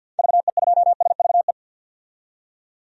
Morse code, heard as a keyed tone, about 50 wpm, 700 Hz. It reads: VE2SVE
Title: DXCC-100 Call Signs - 6 Characters - 50wpm